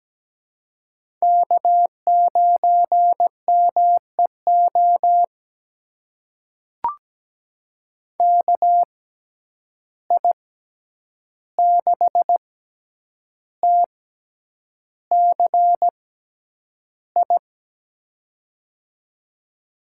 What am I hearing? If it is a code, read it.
Morse code, heard as a keyed tone, about 17 wpm, spaced as for 8 wpm, 700 Hz. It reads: K9MEO KI6TCI
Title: USA Call Signs - 6x Character Spacing - 17wpm